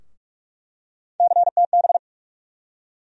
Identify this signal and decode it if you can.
Morse code, heard as a keyed tone, about 47 wpm, 700 Hz. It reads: CTB
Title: USA Call Sign Postfixes - 3 Letters 45wpm